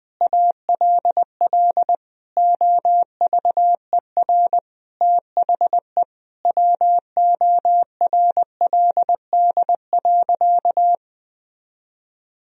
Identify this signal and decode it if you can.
Morse code, heard as a keyed tone, about 21 wpm, 700 Hz. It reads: ALL OVER THE WORLD.
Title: Sentences from Top 1000 Words - Review Part 2 - 20wpm